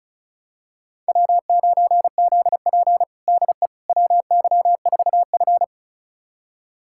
Morse code, heard as a keyed tone, about 36 wpm, 700 Hz. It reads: W9ZP DE WY4F
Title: USA Callsign DE Callsign - 4 Characters - 35wpm